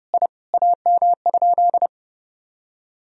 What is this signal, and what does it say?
Morse code, heard as a keyed tone, about 31 wpm, 700 Hz. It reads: I AM?
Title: Sentences from Top 1000 Words - Review Part 3 - 30wpm